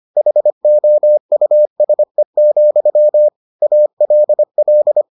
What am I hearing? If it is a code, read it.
Morse code, heard as a keyed tone, about 25 wpm, 600 Hz. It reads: HOUSE, ALL